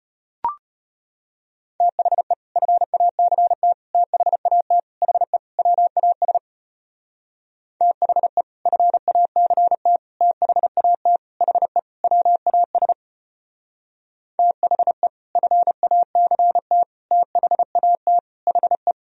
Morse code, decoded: THE FACT THAT HE WAS THE FACT THAT HE WAS THE FACT THAT HE

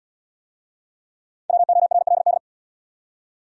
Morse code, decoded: WORLD

